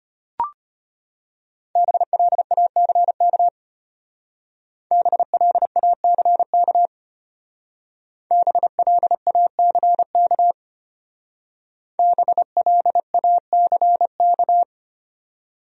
Morse code, heard as a keyed tone, about 32 wpm, 700 Hz. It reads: BLACK BLACK BLACK BLACK